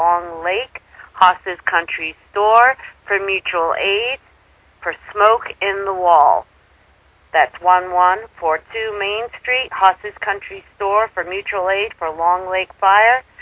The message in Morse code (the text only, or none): none